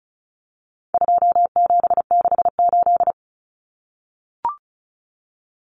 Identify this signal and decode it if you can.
Morse code, heard as a keyed tone, about 34 wpm, 700 Hz. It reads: 2768
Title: Common Cut Numbers - 4-Digit 35wpm